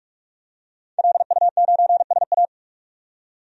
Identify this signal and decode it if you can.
Morse code, heard as a keyed tone, about 46 wpm, 700 Hz. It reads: LU9SA
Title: DXCC-100 Call Signs - 5 Characters - 45wpm